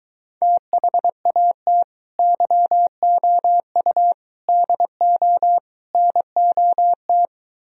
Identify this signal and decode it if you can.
Morse code, heard as a keyed tone, about 22 wpm, 700 Hz. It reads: THAT YOU DO NOT